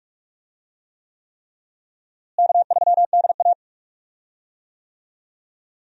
Morse code, heard as a keyed tone, about 46 wpm, 700 Hz. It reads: K3DA
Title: USA Call Signs - 4 Characters - Repeated 3x - 45wpm